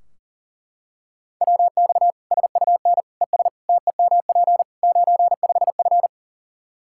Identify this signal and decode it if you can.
Morse code, heard as a keyed tone, about 40 wpm, 700 Hz. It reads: WX SUN ES TEMP 95F
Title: K1USN SST QSO Practice 40wpm